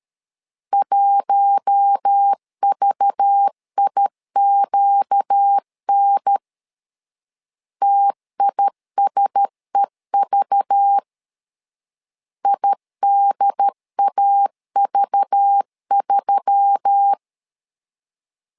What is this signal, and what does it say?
Morse code, heard as a keyed tone, about 13 wpm, 800 Hz. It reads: 1VIQN TISEV IDAV3
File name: LEZIONI / 5 LBVC / 5 LBV 12wpm 1855